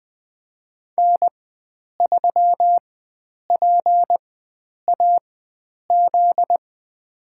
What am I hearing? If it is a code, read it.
Morse code, heard as a keyed tone, about 20 wpm, spaced as for 9 wpm, 700 Hz. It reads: N3PAZ
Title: USA Call Signs - 4x Character Spacing - 20wpm